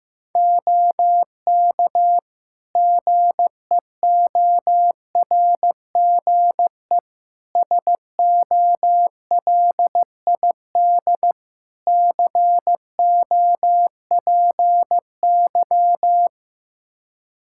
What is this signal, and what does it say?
Morse code, heard as a keyed tone, about 15 wpm, 700 Hz. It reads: OK GEORGE SOLID COPY
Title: K1USN SST QSO Practice 15wpm